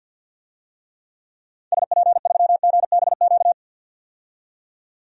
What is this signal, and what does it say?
Morse code, heard as a keyed tone, about 52 wpm, 700 Hz. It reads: SP3GBQ
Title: DXCC-100 Call Signs - 6 Characters - 50wpm